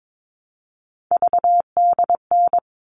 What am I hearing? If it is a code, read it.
Morse code, heard as a keyed tone, about 22 wpm, 700 Hz. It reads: VDN
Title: All Cut Numbers - 3-Digit 22wpm